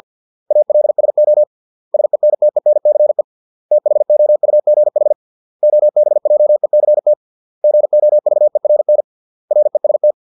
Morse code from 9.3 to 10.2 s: REST